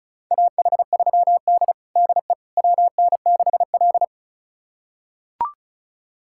Morse code, heard as a keyed tone, about 36 wpm, 700 Hz. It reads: AH3D DE WN6L
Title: USA Callsign DE Callsign - 4 Characters - 35wpm